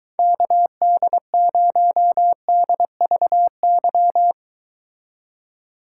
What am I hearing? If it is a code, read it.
Morse code, heard as a keyed tone, about 23 wpm, 700 Hz. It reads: KD0DVY